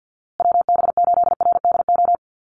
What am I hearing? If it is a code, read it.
Morse code, heard as a keyed tone, about 50 wpm, 700 Hz. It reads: WB8RDO
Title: USA Call Signs Repeated 2x 50wpm